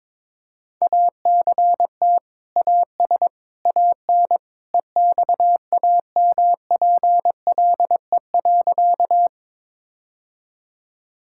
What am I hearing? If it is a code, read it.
Morse code, heard as a keyed tone, about 22 wpm, 700 Hz. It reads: ACT AS AN EXAMPLE.
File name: practice / 22wpm / Sentences with 2-Word Phrases - 22wpm